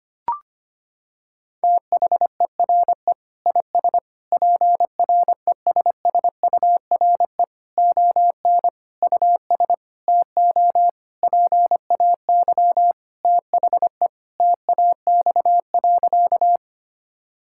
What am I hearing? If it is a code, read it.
Morse code, heard as a keyed tone, about 25 wpm, 700 Hz. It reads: THERE IS PRESSURE ON US TO PAY THE TAX.
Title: Sentences from Top 1000 Words - Review Part 6 - 25wpm